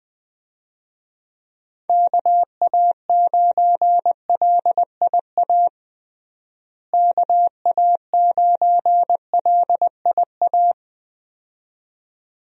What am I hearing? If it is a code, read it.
Morse code, heard as a keyed tone, about 20 wpm, 700 Hz. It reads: KA9LIA KA9LIA
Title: USA Call Signs - 6 Characters - Repeated 2x - 20wpm